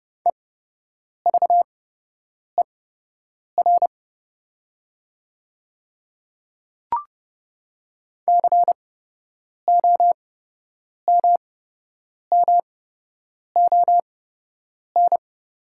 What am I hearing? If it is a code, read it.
Morse code, heard as a keyed tone, about 30 wpm, spaced as for 8 wpm, 700 Hz. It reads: EVER COMMON
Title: 8x Character Spacing - Top 500 Words - 30wpm